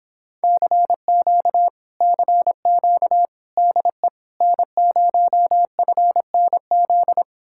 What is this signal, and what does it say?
Morse code, heard as a keyed tone, about 29 wpm, 700 Hz. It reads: CQ CQ DE N0FNZ